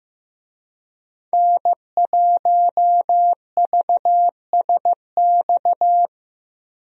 Morse code, heard as a keyed tone, about 15 wpm, 700 Hz. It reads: N1VSX
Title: USA Call Signs - 5 Characters - 15wpm